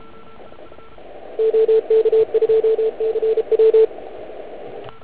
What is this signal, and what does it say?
Morse code, heard as a keyed tone, about 31 wpm, 440 Hz, with radio noise. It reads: OK2CW